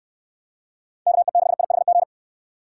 Morse code, heard as a keyed tone, about 68 wpm, 700 Hz. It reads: KE6IHG